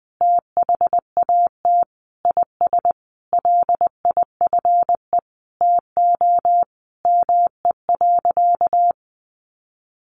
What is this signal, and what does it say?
Morse code, heard as a keyed tone, about 20 wpm, 700 Hz. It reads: THAT IS LIFE TO ME.
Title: Sentences from Top 500 Words 20wpm